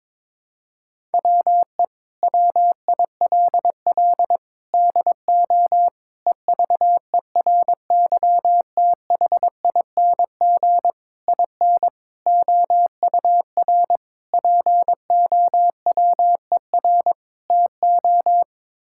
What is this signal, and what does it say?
Morse code, heard as a keyed tone, about 22 wpm, 700 Hz. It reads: WE WILL DO EVERYTHING IN OUR POWER TO
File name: practice / 22wpm / Sets of 8 Words - 22wpm